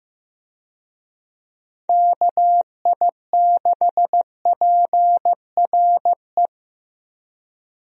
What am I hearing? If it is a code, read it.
Morse code, heard as a keyed tone, about 15 wpm, 700 Hz. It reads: KI6PRE
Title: USA Call Signs - 6 Characters - 15wpm